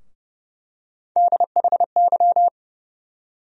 Morse code, heard as a keyed tone, about 31 wpm, 700 Hz. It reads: DHY
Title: USA Call Sign Postfixes - 3 Letters 30wpm